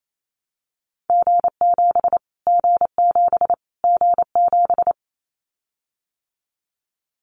Morse code, heard as a keyed tone, about 28 wpm, 700 Hz. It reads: G7 G7 G7
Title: Callsign Prefixes for Top 25 DXCC - Repeated 3x - 28wpm